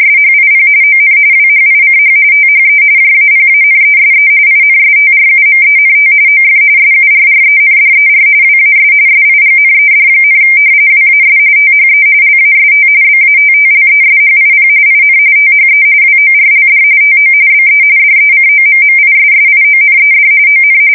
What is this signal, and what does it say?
RTTY, about 45.45 baud. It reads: SHIP ON WHICH THE JAPANESE INSTRUMENT OF SURRENDER WAS SIGNED, WAS THE LAST BATTLESHIP IN THE WORLD TO BE DECOMMISSIONED ON
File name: fsk170 / battleships